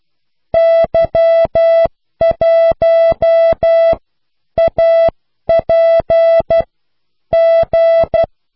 Morse code, heard as a keyed tone, about 12 wpm, spaced as for 9 wpm, 660 Hz. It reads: Y1APG